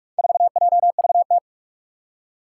Morse code, heard as a keyed tone, about 46 wpm, 700 Hz. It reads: 4JVT
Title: Four Challenging Letter-Number Combinations 45wpm